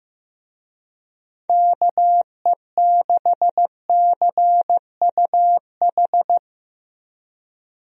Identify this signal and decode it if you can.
Morse code, heard as a keyed tone, about 16 wpm, 700 Hz. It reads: KE6CUH
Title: USA Call Signs - 6 Characters - Repeated 3x - 15wpm